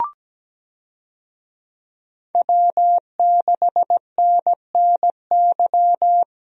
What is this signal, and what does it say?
Morse code, heard as a keyed tone, about 17 wpm, 700 Hz. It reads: W6NNY
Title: USA Call Signs - 5 Characters - Repeated 3x - 17wpm